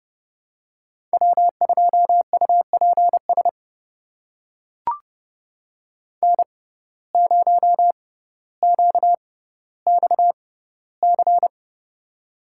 Morse code, heard as a keyed tone, about 30 wpm, spaced as for 15 wpm, 700 Hz. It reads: W2UPS N0QXC
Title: USA Call Signs - 5 Characters - 6x Character Spacing - 30wpm